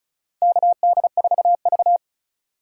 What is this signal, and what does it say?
Morse code, heard as a keyed tone, about 36 wpm, 700 Hz. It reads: KD4V
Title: USA Call Signs - 4 Characters - 3x Character Spacing - 35wpm